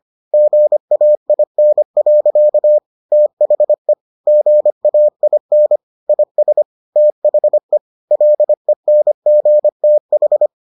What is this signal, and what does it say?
Morse code, heard as a keyed tone, about 25 wpm, 600 Hz. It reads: GAIN. THE GAIN IS THE LENGTH